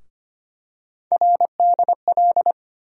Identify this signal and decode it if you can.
Morse code, heard as a keyed tone, about 25 wpm, 700 Hz. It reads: RDL
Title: USA Call Sign Postfixes - 3 Letters 25wpm